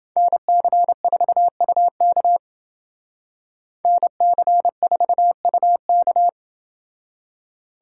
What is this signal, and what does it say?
Morse code, heard as a keyed tone, about 29 wpm, 700 Hz. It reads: NC4UK NC4UK